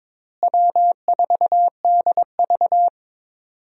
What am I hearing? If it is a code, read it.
Morse code, heard as a keyed tone, about 21 wpm, 700 Hz. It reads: W4DV
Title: USA Call Signs - 4 Characters - Repeated 2x - 22wpm